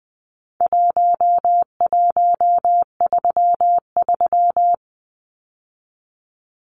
Morse code, heard as a keyed tone, about 20 wpm, 700 Hz. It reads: 1133
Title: Numbers - Contest Style - 4-Digit 20wpm